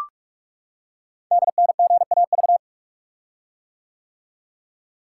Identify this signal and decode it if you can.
Morse code, heard as a keyed tone, about 44 wpm, 700 Hz. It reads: DNGAV
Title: Code Groups - 5 Letters - 45wpm